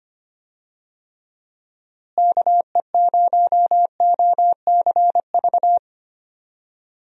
Morse code, heard as a keyed tone, about 25 wpm, 700 Hz. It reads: KE0OCV
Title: USA Call Signs - 6 Characters - Repeated 2x - 25wpm